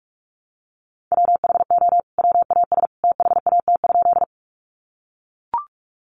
Morse code, heard as a keyed tone, about 44 wpm, 700 Hz. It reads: WHO WAS THAT?
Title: Sentences from Top 100 Words 45wpm